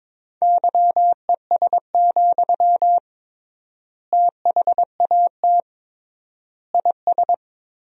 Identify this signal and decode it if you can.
Morse code, heard as a keyed tone, about 22 wpm, 700 Hz. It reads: YES, THAT IS
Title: 3x Word-Spacing - Sentences from Top 500 Words - 22wpm